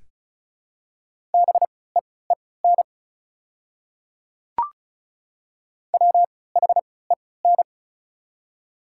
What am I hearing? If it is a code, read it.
Morse code, heard as a keyed tone, about 36 wpm, spaced as for 20 wpm, 690 Hz. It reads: BEEN WHEN
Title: 3x Character Spacing - Top 300 Words - 35wpm